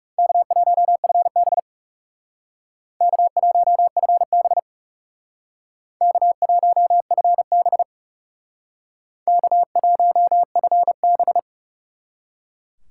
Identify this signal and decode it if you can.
Morse code, heard as a keyed tone, about 36 wpm, 700 Hz. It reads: K1FB K1FB K1FB K1FB
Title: Speed Racing - USA Call Signs 30wpm